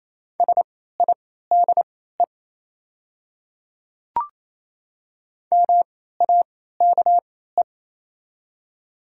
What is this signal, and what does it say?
Morse code, heard as a keyed tone, about 28 wpm, spaced as for 16 wpm, 700 Hz. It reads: SIDE MAKE
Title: 3x Character Spacing - Top 200 Words - 28wpm